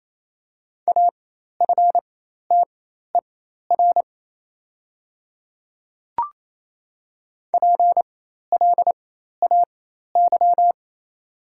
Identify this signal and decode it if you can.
Morse code, heard as a keyed tone, about 27 wpm, spaced as for 13 wpm, 700 Hz. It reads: AFTER PLAY
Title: 4x Character Spacing - Top 200 Words - 28wpm